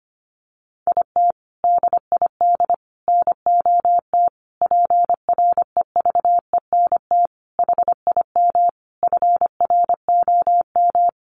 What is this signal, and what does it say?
Morse code, heard as a keyed tone, about 25 wpm, 700 Hz. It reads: IT DID NOT PREVENT HIM FROM
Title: Sentences from Top 1500 Words 25wpm